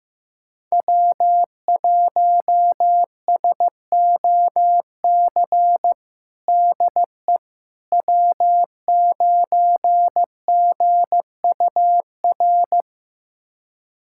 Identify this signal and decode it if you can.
Morse code, heard as a keyed tone, about 15 wpm, 700 Hz. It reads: W1SOC DE W9GUR
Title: USA Callsign DE Callsign - 5 Characters - 15wpm